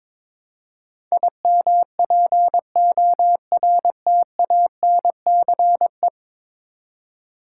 Morse code, heard as a keyed tone, about 22 wpm, 700 Hz. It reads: IMPORTANCE
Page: morse-code-ninja-practice-22wpm